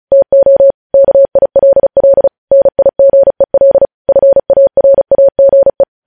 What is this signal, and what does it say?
Morse code, heard as a keyed tone, about 36 wpm, 550 Hz. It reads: TO KILL NIGEL FARAGE
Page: news headlines in morse code at 35 wpm